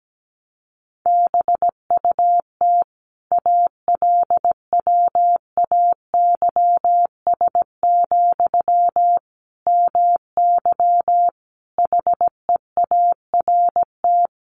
Morse code, heard as a keyed tone, about 17 wpm, 700 Hz. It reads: BUT ALWAYS, MY HEART
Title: Sentences from Top 1200 Words 17wpm